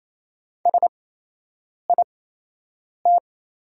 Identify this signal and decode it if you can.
Morse code, heard as a keyed tone, about 28 wpm, spaced as for 8 wpm, 700 Hz. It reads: SIT
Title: 8x Character Spacing - Top 500 Words - 28wpm